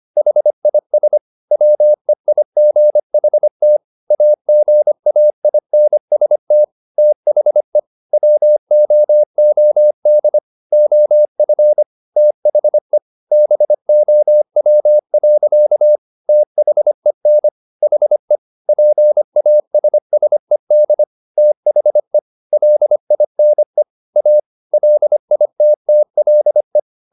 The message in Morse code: HIS WEIGHT AGAINST THE WOOD OF THE BOW. THEN HE PASSED THE LINE A LITTLE

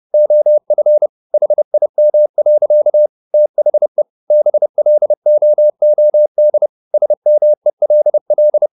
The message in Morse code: OF HIM. THE BLOOD SMELL

